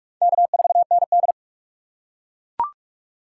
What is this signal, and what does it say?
Morse code, heard as a keyed tone, about 42 wpm, 700 Hz. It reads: K4ND
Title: USA Call Signs - 4 Characters - 2x Character Spacing - 45wpm